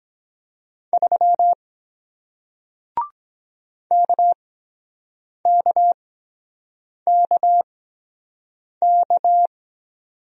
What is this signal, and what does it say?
Morse code, heard as a keyed tone, about 23 wpm, spaced as for 7 wpm, 700 Hz. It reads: 3 KKKK